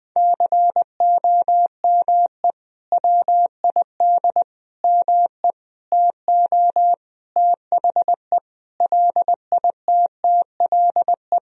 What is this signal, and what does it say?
Morse code, heard as a keyed tone, about 20 wpm, 700 Hz. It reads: COME WID ME TO THE LITTLE
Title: Sentences from Top 200 Words and Abbreviations 20wpm